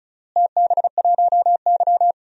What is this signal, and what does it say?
Morse code, heard as a keyed tone, about 36 wpm, 690 Hz. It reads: TB1Y
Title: DXCC-100 Call Signs - 4 Characters - Repeated 3x - 35wpm